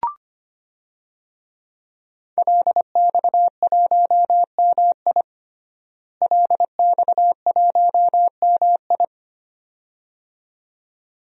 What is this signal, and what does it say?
Morse code, heard as a keyed tone, about 25 wpm, 700 Hz. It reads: LX1MI LX1MI
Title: DXCC-100 Call Signs - 5 Characters - Repeated 2x - 25wpm